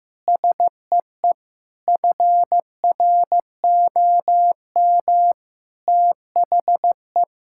Morse code, heard as a keyed tone, about 15 wpm, 700 Hz. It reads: SEE FROM THE
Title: Sentences from Top 1600 Words 15wpm